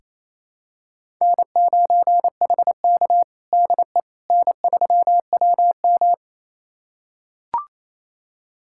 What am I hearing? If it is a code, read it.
Morse code, heard as a keyed tone, about 28 wpm, 700 Hz. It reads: N9HK DE N3WM
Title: USA Callsign DE Callsign - 4 Characters - 28wpm